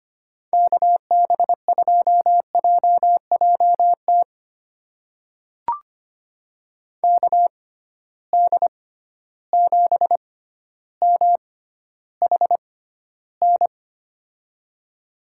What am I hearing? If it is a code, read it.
Morse code, heard as a keyed tone, about 25 wpm, spaced as for 13 wpm, 700 Hz. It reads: KB2JJT KD7MHN
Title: USA Call Signs - 6 Characters - 6x Character Spacing - 25wpm